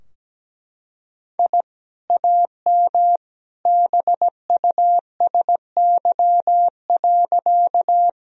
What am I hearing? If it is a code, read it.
Morse code, heard as a keyed tone, about 17 wpm, 700 Hz. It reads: I AM BUSY.
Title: Sentences from Top 1000 Words - Review Part 3 - 17wpm